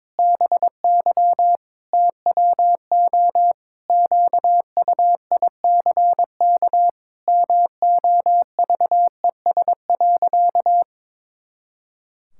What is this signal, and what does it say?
Morse code, heard as a keyed tone, about 22 wpm, 700 Hz. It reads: BY TWO QUICK MOVES.
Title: Pangram Challenge v2 - 22wpm